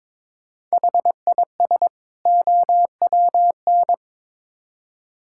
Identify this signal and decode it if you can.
Morse code, heard as a keyed tone, about 23 wpm, 700 Hz. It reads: HIS OWN
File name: practice / 22wpm / Build A Phrase - 2-Word Phrases - 22wpm